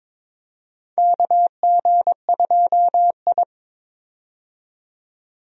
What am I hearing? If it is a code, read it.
Morse code, heard as a keyed tone, about 21 wpm, 700 Hz. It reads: KG2I
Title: USA Call Signs - 4 Characters - 22wpm